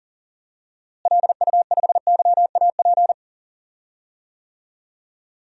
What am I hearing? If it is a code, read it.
Morse code, heard as a keyed tone, about 40 wpm, 700 Hz. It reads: LU5YAP
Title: DXCC-100 Call Signs - 6 Characters - Repeated 3x - 40wpm